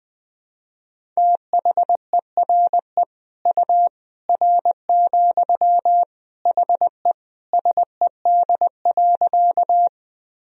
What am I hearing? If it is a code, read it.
Morse code, heard as a keyed tone, about 20 wpm, 700 Hz. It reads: THERE U R, HE SED.
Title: Sentences from Top 100 Words and Abbreviations 20wpm